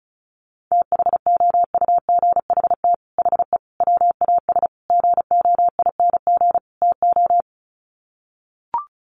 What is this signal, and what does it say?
Morse code, heard as a keyed tone, about 34 wpm, 700 Hz. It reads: THOUGHT HE WAS GOING TO
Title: Sets of 5 Words Encore2 35wpm